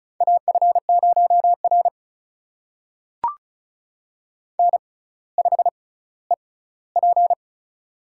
Morse code, heard as a keyed tone, about 36 wpm, spaced as for 18 wpm, 700 Hz. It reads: AF0R N5EP
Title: USA Call Signs - 4 Characters - 6x Character Spacing - 35wpm